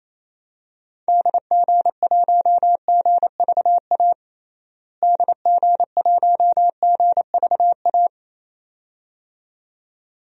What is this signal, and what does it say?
Morse code, heard as a keyed tone, about 28 wpm, 700 Hz. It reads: DG1GVA DG1GVA
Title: DXCC-100 Call Signs - 6 Characters - Repeated 2x - 28wpm